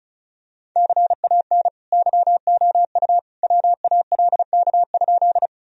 Morse code, heard as a keyed tone, about 36 wpm, 700 Hz. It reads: CAN YOU WALK?
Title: Sentences from Top 1000 Words - Review Part 1 - 35wpm